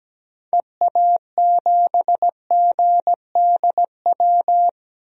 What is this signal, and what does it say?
Morse code, heard as a keyed tone, about 17 wpm, 700 Hz. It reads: EA7GDW
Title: DXCC-100 Call Signs - 6 Characters - Repeated 2x - 17wpm